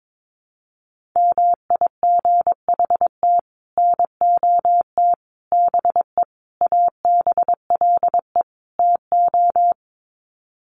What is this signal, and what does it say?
Morse code, heard as a keyed tone, about 22 wpm, 700 Hz. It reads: MIGHT NOT BE ABLE TO
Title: Sets of 5 Words Encore2 22wpm